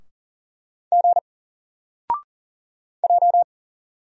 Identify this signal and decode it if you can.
Morse code, heard as a keyed tone, about 38 wpm, 700 Hz. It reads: G J
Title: Single Letter-Number 40wpm